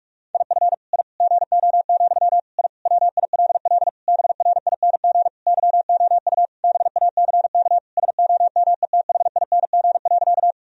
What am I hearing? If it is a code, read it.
Morse code, heard as a keyed tone, about 46 wpm, 700 Hz. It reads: IF I GO, I WILL BRING YOU BACK SOMETHING.
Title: Sentences from Top 1000 Words - Review Part 3 - 45wpm